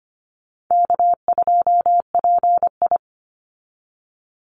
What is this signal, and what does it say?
Morse code, heard as a keyed tone, about 25 wpm, 700 Hz. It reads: K2PI
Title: Call Signs - 25wpm